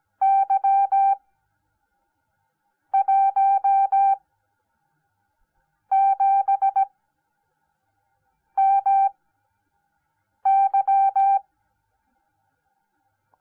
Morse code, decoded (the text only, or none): Y17MY